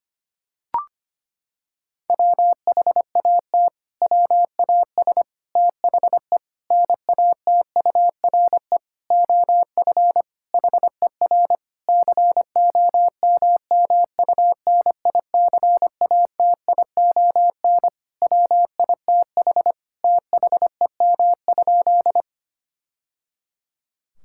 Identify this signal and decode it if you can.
Morse code, heard as a keyed tone, about 25 wpm, 700 Hz. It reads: WHAT WAS THE NATURE OF HER COMMUNICATION WITH THEM?
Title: Sentences from Top 1000 Words - Review Part 6 - 25wpm